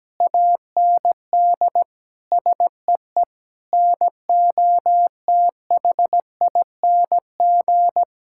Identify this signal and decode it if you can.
Morse code, heard as a keyed tone, about 17 wpm, 700 Hz. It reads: AND SEE NOTHING
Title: Sentences from Top 1000 Words - Review Part 6 - 17wpm